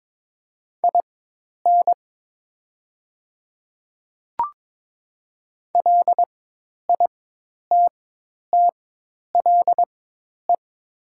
Morse code, heard as a keyed tone, about 22 wpm, spaced as for 10 wpm, 700 Hz. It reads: IN LITTLE
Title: 4x Character Spacing - Top 200 Words - 22wpm